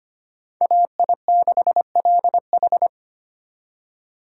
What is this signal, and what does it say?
Morse code, heard as a keyed tone, about 31 wpm, 700 Hz. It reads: AI6LH
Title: Speed Racing - USA 2x2 Call Signs 25wpm